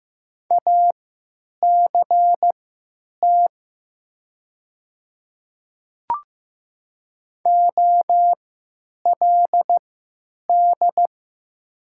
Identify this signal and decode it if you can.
Morse code, heard as a keyed tone, about 15 wpm, spaced as for 9 wpm, 700 Hz. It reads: ACT OLD
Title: 3x Character Spacing - Top 200 Words - 15wpm